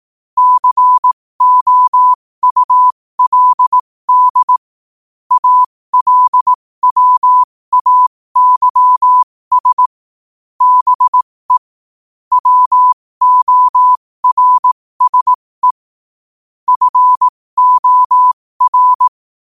Morse code, heard as a keyed tone, about 18 wpm, spaced as for 15 wpm, 1000 Hz. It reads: COULD ALWAYS BE WORSE FOR
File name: qotdfiles / 15wpm